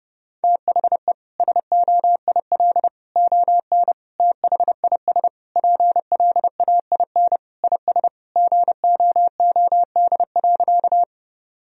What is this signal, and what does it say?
Morse code, heard as a keyed tone, about 30 wpm, 700 Hz. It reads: THE SOIL ON THIS PLAIN IS GOOD.
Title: Sentences from Top 1000 Words - Review Part 4 - 30wpm